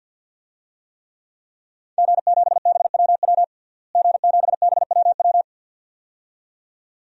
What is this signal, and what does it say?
Morse code, heard as a keyed tone, about 52 wpm, 700 Hz. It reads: G7BWW G7BWW
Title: DXCC-100 Call Signs - 5 Characters - Repeated 2x - 50wpm